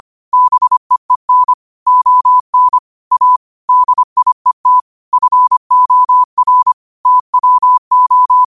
Morse code, heard as a keyed tone, about 25 wpm, 1000 Hz. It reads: BEEN ON A DIET FOR TWO
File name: qotdfiles / 25wpm